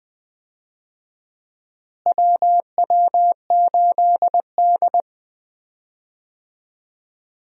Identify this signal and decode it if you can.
Morse code, heard as a keyed tone, about 20 wpm, 700 Hz. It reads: WW8D
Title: USA Call Signs - 4 Characters - Repeated 2x - 20wpm